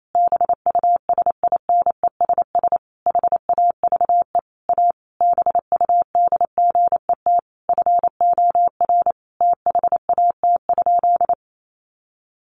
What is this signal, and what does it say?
Morse code, heard as a keyed tone, about 28 wpm, 700 Hz. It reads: BUSINESS HAVE A BUDGET FOR THAT?
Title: Sentences from Top 1200 Words 28wpm